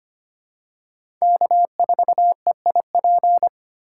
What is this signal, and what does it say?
Morse code, heard as a keyed tone, about 25 wpm, 700 Hz. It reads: K4EIP
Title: USA Call Signs - 4x Character Spacing - 25wpm